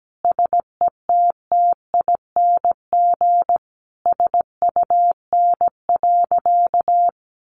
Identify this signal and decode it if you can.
Morse code, heard as a keyed tone, about 17 wpm, 700 Hz. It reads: SETTING SUN.
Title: Sentences from Top 1300 Words 17wpm